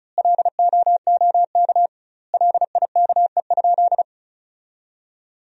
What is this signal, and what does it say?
Morse code, heard as a keyed tone, about 34 wpm, 700 Hz. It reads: LOOK LIKE?
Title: 2x Word-Spacing - Sentences from Top 100 Words - 35wpm